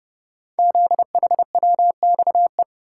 Morse code, heard as a keyed tone, about 30 wpm, 700 Hz. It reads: ZHWXE
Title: Code Groups - 5 Letters - 30wpm